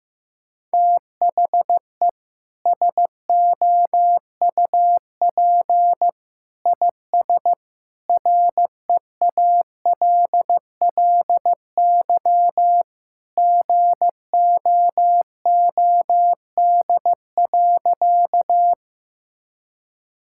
Morse code, heard as a keyed tone, about 15 wpm, 700 Hz. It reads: THE SOUP IS REALLY GOOD.
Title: Sentences from Top 1000 Words - Review Part 6 - 15wpm